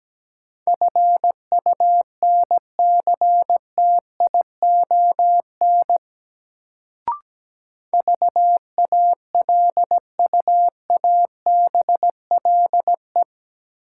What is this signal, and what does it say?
Morse code, heard as a keyed tone, about 17 wpm, 700 Hz. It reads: FUNCTION VALUABLE
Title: Sentences from Top 1100 Words - New Word Drill - 17wpm